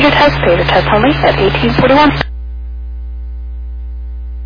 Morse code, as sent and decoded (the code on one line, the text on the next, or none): none